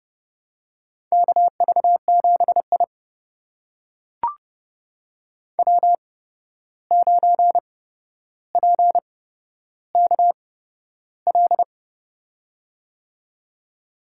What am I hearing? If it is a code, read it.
Morse code, heard as a keyed tone, about 30 wpm, spaced as for 12 wpm, 700 Hz. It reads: KV7I W9PKL
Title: USA Call Signs - 8x Character Spacing - 30wpm